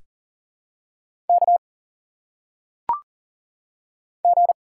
Morse code, heard as a keyed tone, about 40 wpm, 700 Hz. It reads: K G
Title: Single Letters - Phonetic Pronunciation 40wpm